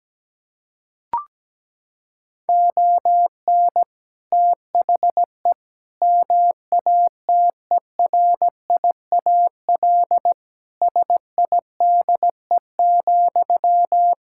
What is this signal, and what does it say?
Morse code, heard as a keyed tone, about 17 wpm, 700 Hz. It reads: ON THE MATERIAL SIDE,
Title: Sentences from Top 1000 Words - Review Part 5 - 17wpm